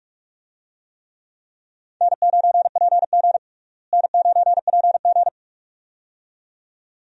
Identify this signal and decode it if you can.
Morse code, heard as a keyed tone, about 46 wpm, 700 Hz. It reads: N9PG N9PG
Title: USA Call Signs - 4 Characters - Repeated 2x - 45wpm